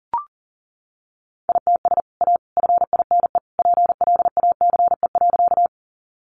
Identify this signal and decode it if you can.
Morse code, heard as a keyed tone, about 40 wpm, 700 Hz. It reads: ITS A FINE PLACE.
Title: Sentences from Top 600 Words 40wpm